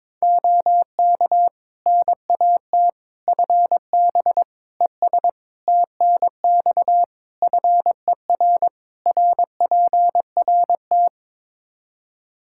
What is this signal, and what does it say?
Morse code, heard as a keyed tone, about 22 wpm, 700 Hz. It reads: OK NAT FB ES TNX FER RPRT
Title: Practice QSO - Exchange 2 - Info - 22wpm